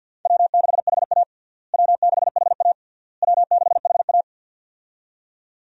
Morse code, heard as a keyed tone, about 50 wpm, 700 Hz. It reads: W6HA W6HA W6HA
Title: USA Call Signs - 4 Characters - Repeated 3x - 50wpm